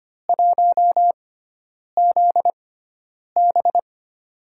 Morse code, heard as a keyed tone, about 25 wpm, spaced as for 9 wpm, 700 Hz. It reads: 1ZB
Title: USA Call Signs - 5 Characters - 6x Character Spacing - 25wpm